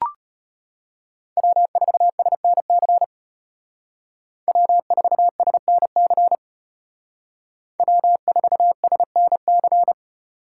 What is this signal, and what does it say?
Morse code, heard as a keyed tone, about 34 wpm, 700 Hz. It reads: W4SNC W4SNC W4SNC